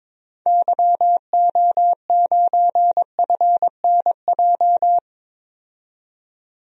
Morse code, heard as a keyed tone, about 22 wpm, 700 Hz. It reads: YO9FNJ
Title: DXCC-100 Call Signs - 6 Characters - 22wpm